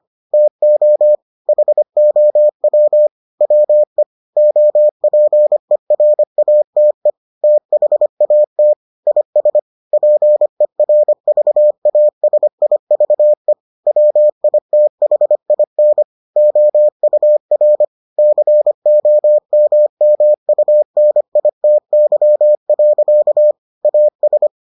Morse code, heard as a keyed tone, about 25 wpm, 600 Hz. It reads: TO HOW WE OPERATE THAT IS PERVASIVE WITHIN OUR COMMUNITY. AS